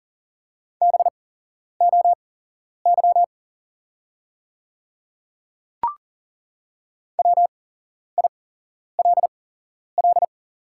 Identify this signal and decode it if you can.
Morse code, heard as a keyed tone, about 40 wpm, spaced as for 11 wpm, 700 Hz. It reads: BOY WILL
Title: 8x Character Spacing - Top 200 Words - 40wpm